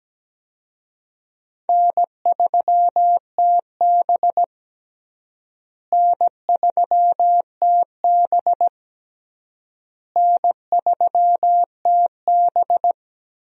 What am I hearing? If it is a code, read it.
Morse code, heard as a keyed tone, about 17 wpm, 700 Hz. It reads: N3TB N3TB N3TB
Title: USA Call Signs - 4 Characters - Repeated 3x - 17wpm